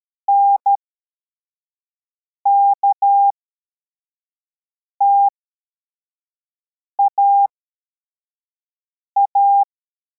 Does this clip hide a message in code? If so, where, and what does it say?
Yes: Morse code from 0.3 to 9.6 s, heard as a keyed tone, about 13 wpm, spaced as for 4 wpm, 800 Hz. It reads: NKTAA